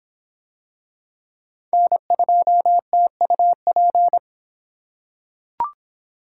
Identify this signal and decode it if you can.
Morse code, heard as a keyed tone, about 22 wpm, 700 Hz. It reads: N2TUP